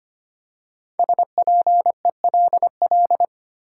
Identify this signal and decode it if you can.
Morse code, heard as a keyed tone, about 25 wpm, 700 Hz. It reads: SPELL
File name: 5 Letter Words - First 3 Letters Given - 25wpm